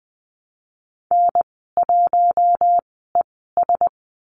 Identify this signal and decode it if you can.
Morse code, heard as a keyed tone, about 20 wpm, spaced as for 14 wpm, 700 Hz. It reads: N1ES
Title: 069 Lesson 12g - Letter d - Call Signs 20wpm